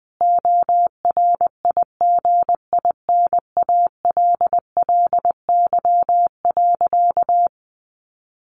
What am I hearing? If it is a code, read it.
Morse code, heard as a keyed tone, about 20 wpm, 700 Hz. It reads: ORIGINALLY.
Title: Sentences from Top 1500 Words 20wpm